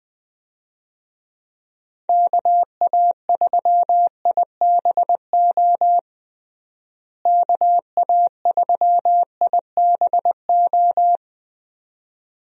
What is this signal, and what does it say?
Morse code, heard as a keyed tone, about 20 wpm, 700 Hz. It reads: KA3IBO KA3IBO
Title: USA Call Signs - 6 Characters - Repeated 2x - 20wpm